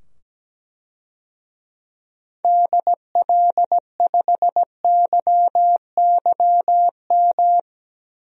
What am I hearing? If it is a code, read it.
Morse code, heard as a keyed tone, about 17 wpm, 700 Hz. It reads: DL5YYM
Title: DXCC-100 Call Signs - 6 Characters - 17wpm